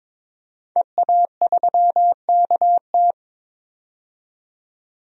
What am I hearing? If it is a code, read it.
Morse code, heard as a keyed tone, about 21 wpm, 700 Hz. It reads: EA3KT
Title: DXCC-100 Call Signs - 5 Characters - 22wpm